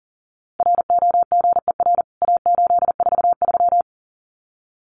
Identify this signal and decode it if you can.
Morse code, heard as a keyed tone, about 37 wpm, 700 Hz. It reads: ROGER A843
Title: CWT Exchange 40wpm